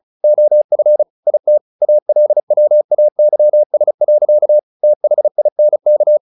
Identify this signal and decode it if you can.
Morse code, heard as a keyed tone, about 34 wpm, 600 Hz. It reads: OF IT ALWAYS. THINK
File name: the old man and the sea / the old man and the sea 1 35wpm